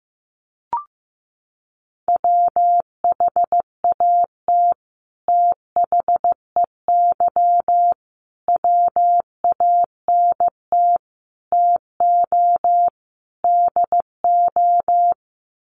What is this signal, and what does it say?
Morse code, heard as a keyed tone, about 15 wpm, 700 Hz. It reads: WHAT THEY WANT TO DO
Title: Sets of 5 Words Encore2 15wpm